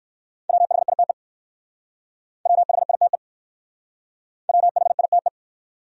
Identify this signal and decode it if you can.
Morse code, heard as a keyed tone, about 54 wpm, 700 Hz. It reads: WHITE WHITE WHITE
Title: Speed-Racing Top 300 Words 45wpm